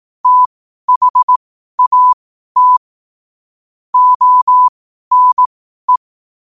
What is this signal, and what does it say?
Morse code, heard as a keyed tone, about 18 wpm, spaced as for 12 wpm, 1000 Hz. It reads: THAT ONE
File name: qotdfiles / 12wpm